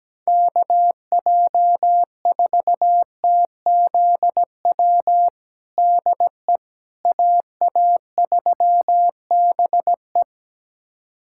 Morse code, heard as a keyed tone, about 17 wpm, 700 Hz. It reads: KJ4TZW DE AA3BE
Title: USA Callsign DE Callsign 17wpm